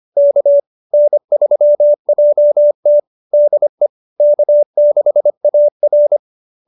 Morse code, heard as a keyed tone, about 25 wpm, 590 Hz. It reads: K N3JT DE K6AR